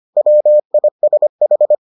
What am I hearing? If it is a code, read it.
Morse code, heard as a keyed tone, about 25 wpm, 600 Hz. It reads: WISH